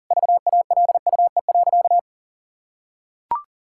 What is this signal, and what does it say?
Morse code, heard as a keyed tone, about 40 wpm, 700 Hz. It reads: VALUE.
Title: Sentences from Top 1000 Words - Review Part 6 - 40wpm